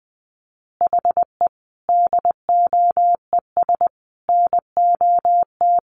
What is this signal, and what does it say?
Morse code, heard as a keyed tone, about 20 wpm, 700 Hz. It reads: HE DOES NOT
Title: Additional Easy Phrases 20wpm